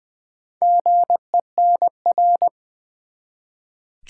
Morse code, heard as a keyed tone, about 20 wpm, 700 Hz. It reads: GENR
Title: Build A Word - 1 Letter At A Time - 3 to 5-Letter Words - 20wpm